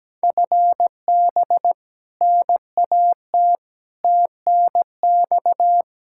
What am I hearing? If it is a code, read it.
Morse code, heard as a keyed tone, about 17 wpm, 700 Hz. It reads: FB NAT TNX
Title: Practice QSO - Exchange 3 - More Info - 17wpm